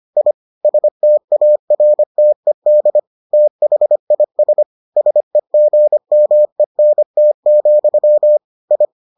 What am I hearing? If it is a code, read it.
Morse code, heard as a keyed tone, about 25 wpm, 600 Hz. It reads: I STARTED THIS SEGMENT, I